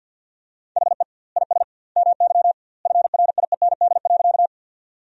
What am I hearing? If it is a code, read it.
Morse code, heard as a keyed tone, about 52 wpm, 700 Hz. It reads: HE IS MY FRIEND.
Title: 2x Word-Spacing - Sentences from Top 300 Words - 50wpm